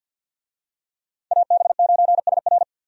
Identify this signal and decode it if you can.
Morse code, heard as a keyed tone, about 52 wpm, 700 Hz. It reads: AB9SR
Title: USA Call Signs - 5 Characters - Repeated 2x - 50wpm